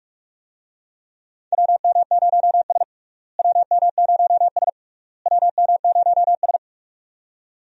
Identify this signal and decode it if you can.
Morse code, heard as a keyed tone, about 46 wpm, 700 Hz. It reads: WM0S WM0S WM0S
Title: USA Call Signs - 4 Characters - Repeated 3x - 45wpm